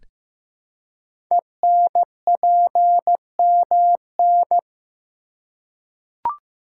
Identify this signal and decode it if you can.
Morse code, heard as a keyed tone, about 15 wpm, 700 Hz. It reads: ENPMN
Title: Code Groups - 5 Letter:Numbers - 15wpm